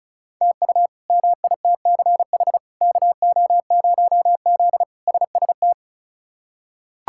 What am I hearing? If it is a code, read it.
Morse code, heard as a keyed tone, about 36 wpm, 700 Hz. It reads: TU MITCH KO0Z SST
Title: K1USN SST QSO Practice - 35wpm